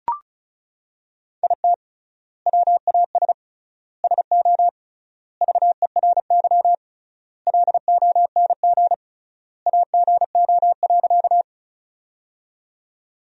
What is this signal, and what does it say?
Morse code, heard as a keyed tone, about 36 wpm, 700 Hz. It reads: IT WAS SO VERY LONG AGO.
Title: 3x Word-Spacing - Sentences from Top 500 Words - 35wpm